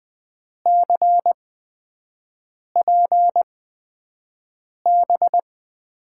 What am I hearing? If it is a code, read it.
Morse code, heard as a keyed tone, about 21 wpm, spaced as for 5 wpm, 700 Hz. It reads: CPB
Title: USA Call Signs - 6 Characters - 8x Character Spacing - 20wpm